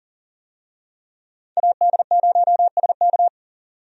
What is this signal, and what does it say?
Morse code, heard as a keyed tone, about 40 wpm, 700 Hz. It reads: AD0SK
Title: USA Call Signs - 5 Characters - 40wpm